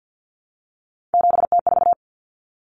Morse code, heard as a keyed tone, about 50 wpm, 700 Hz. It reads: 7T4
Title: Common Cut Numbers - 3-Digit 50wpm